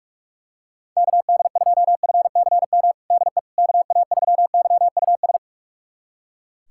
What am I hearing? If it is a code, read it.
Morse code, heard as a keyed tone, about 41 wpm, 700 Hz. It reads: KD2FCM DE KA3YUS
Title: USA Callsign DE Callsign - 6 Characters - 45wpm